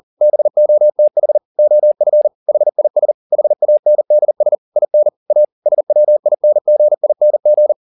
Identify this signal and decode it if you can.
Morse code, heard as a keyed tone, about 40 wpm, 600 Hz. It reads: BOTH OF HIS HANDS IN A SWINGING
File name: the old man and the sea 2 40wpm